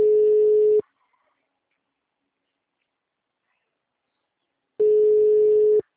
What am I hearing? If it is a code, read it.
Ringback tone (Europe, 425 Hz).
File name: out- -2036